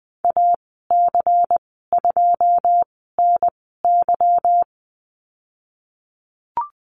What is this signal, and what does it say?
Morse code, heard as a keyed tone, about 20 wpm, spaced as for 15 wpm, 700 Hz. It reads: AC2NY
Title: 167 Lesson 27g - Number 7 - Call Signs 20wpm